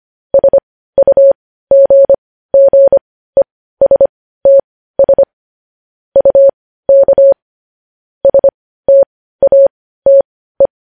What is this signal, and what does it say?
Morse code, heard as a keyed tone, about 25 wpm, spaced as for 15 wpm, 550 Hz. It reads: SUGGESTS UK STATE
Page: News Headlines in Morse Code at 15 WPM